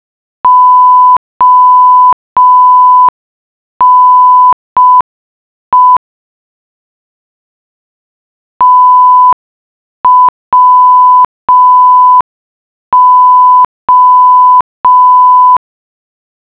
Morse code, heard as a keyed tone, about 5 wpm, 1000 Hz. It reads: ONE TWO